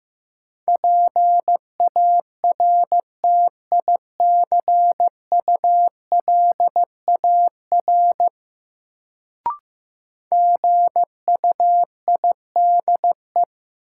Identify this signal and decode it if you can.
Morse code, heard as a keyed tone, about 15 wpm, 700 Hz. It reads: PARTICULAR GUIDE
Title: Sentences from Top 600 Words - New Word Drill - 15wpm